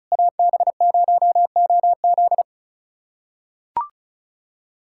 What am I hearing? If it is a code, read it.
Morse code, heard as a keyed tone, about 36 wpm, 700 Hz. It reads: AB0OZ
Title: USA Callsign DE Callsign - 5 Characters - 35wpm